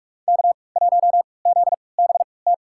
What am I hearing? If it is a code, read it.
Morse code, heard as a keyed tone, about 46 wpm, spaced as for 25 wpm, 700 Hz. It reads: K1ZBT
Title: USA Call Signs - 5 Characters - 3x Character Spacing - 45wpm